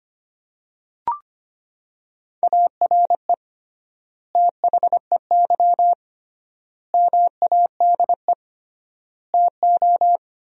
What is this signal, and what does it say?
Morse code, heard as a keyed tone, about 25 wpm, 700 Hz. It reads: ARE THEY MADE TO
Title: 3x Word-Spacing - Sentences from Top 400 Words - 25wpm